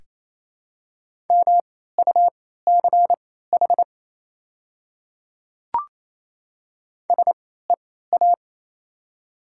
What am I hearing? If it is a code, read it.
Morse code, heard as a keyed tone, about 28 wpm, spaced as for 16 wpm, 700 Hz. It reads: MUCH SEA
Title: 3x Character Spacing - Top 300 Words - 28wpm